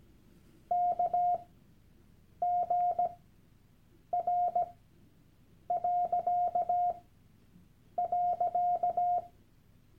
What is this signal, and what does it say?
Morse code, heard as a keyed tone, about 17 wpm, spaced as for 7 wpm, 710 Hz. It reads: KGR..
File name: Workshop Dateien / CW-Online-Kurs / Lektion 001